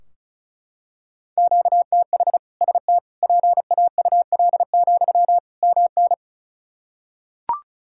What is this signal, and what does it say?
Morse code, heard as a keyed tone, about 36 wpm, 700 Hz. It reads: QTH ST PAUL, MN
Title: K1USN SST QSO Practice 35wpm